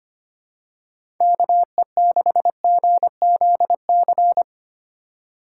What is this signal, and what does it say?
Morse code, heard as a keyed tone, about 25 wpm, 700 Hz. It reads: KE6GZC
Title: USA Call Signs - 6 Characters - 4x Character Spacing - 25wpm